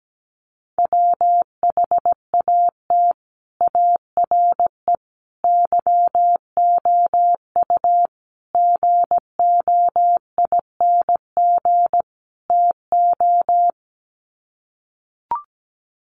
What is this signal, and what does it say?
Morse code, heard as a keyed tone, about 17 wpm, 700 Hz. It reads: WHAT ARE YOU GOING TO
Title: Sets of 5 Words 17wpm